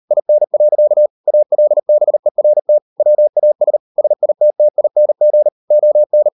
Morse code, heard as a keyed tone, about 38 wpm, 600 Hz. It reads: IN. ALBERT WAS SITTING ON